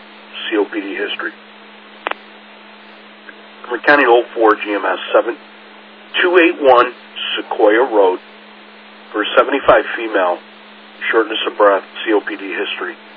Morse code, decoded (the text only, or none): none